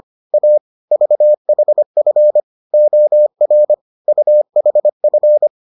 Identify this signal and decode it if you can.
Morse code, heard as a keyed tone, about 25 wpm, 600 Hz. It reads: A VHF OR UHF